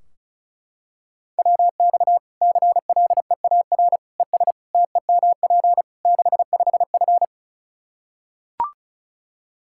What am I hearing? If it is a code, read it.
Morse code, heard as a keyed tone, about 34 wpm, 700 Hz. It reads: WX CLEAR ES TEMP 65F
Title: Practice QSO - Exchange 2 - Info - 35wpm